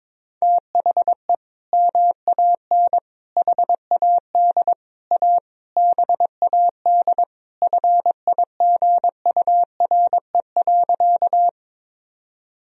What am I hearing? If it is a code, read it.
Morse code, heard as a keyed tone, about 21 wpm, 700 Hz. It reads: THE MAN HAD A BAD FIGURE.
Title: Sentences from Top 1000 Words - Review Part 2 - 22wpm